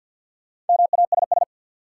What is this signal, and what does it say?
Morse code, heard as a keyed tone, about 48 wpm, 700 Hz. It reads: MASS
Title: Sets of 5 Words Encore - New Word Drill - 50wpm